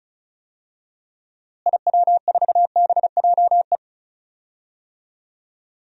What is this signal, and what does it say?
Morse code, heard as a keyed tone, about 36 wpm, 700 Hz. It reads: IW4BJE
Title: DXCC-100 Call Signs - 6 Characters - Repeated 2x - 35wpm